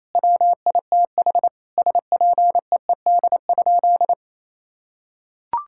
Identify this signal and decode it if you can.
Morse code, heard as a keyed tone, about 27 wpm, 700 Hz. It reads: WITH SPEED?
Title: Sentences from Top 1000 Words - Review Part 5 - 28wpm